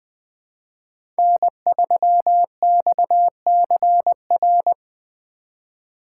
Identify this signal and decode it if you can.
Morse code, heard as a keyed tone, about 20 wpm, 700 Hz. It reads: N3XCR
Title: USA Call Signs - 5 Characters - 4x Character Spacing - 20wpm